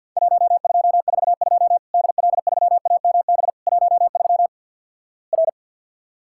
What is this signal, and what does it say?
Morse code, heard as a keyed tone, about 50 wpm, 700 Hz. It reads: 1242 DL3AMB 13 R
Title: Worked All Europe DX Contest - QTC Exchange with Repeats - 50wpm